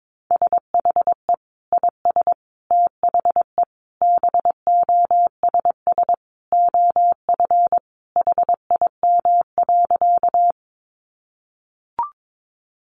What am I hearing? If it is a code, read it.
Morse code, heard as a keyed tone, about 22 wpm, 700 Hz. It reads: SHE IS THE BOSS OF HIM.
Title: Sentences from Top 700 Words 22wpm